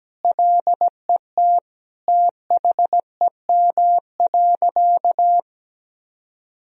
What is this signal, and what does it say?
Morse code, heard as a keyed tone, about 17 wpm, 700 Hz. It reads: LET THEM.
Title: Sentences from Top 1000 Words - Review Part 2 - 17wpm